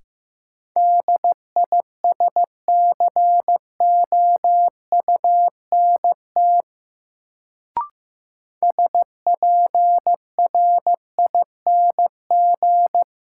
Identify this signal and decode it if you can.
Morse code, heard as a keyed tone, about 15 wpm, 700 Hz. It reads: DISCOUNT SPRING
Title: Sentences from Top 1200 Words - New Word Drill Alt - 15wpm